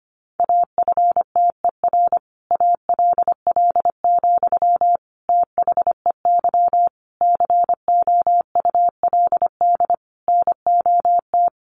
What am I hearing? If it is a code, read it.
Morse code, heard as a keyed tone, about 25 wpm, 700 Hz. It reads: AFTER ALL, THEY COULD NOT
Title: Sentences from Top 1100 Words 25wpm